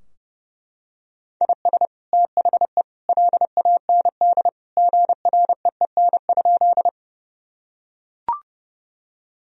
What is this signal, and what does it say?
Morse code, heard as a keyed tone, about 30 wpm, 700 Hz. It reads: IS THE LAND GREEN?
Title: Sentences from Top 1000 Words - Review Part 2 - 30wpm